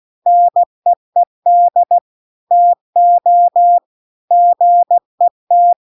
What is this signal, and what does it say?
Morse code, heard as a keyed tone, about 16 wpm, 700 Hz. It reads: NEED TO GET